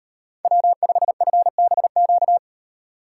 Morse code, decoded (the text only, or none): W5FBQ